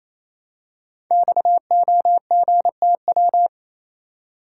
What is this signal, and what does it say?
Morse code, heard as a keyed tone, about 27 wpm, 700 Hz. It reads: XOGTW
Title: Code Groups - 5 Letter:Numbers - 28wpm